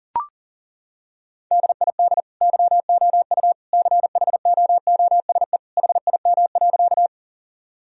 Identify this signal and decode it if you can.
Morse code, heard as a keyed tone, about 40 wpm, 700 Hz. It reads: DID YOU CHOOSE HIM.
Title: Sentences from Top 1000 Words - Review Part 4 - 40wpm